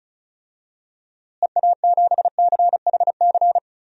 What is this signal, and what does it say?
Morse code, heard as a keyed tone, about 36 wpm, 700 Hz. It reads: EA7CHC
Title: DXCC-100 Call Signs - 6 Characters - Repeated 3x - 35wpm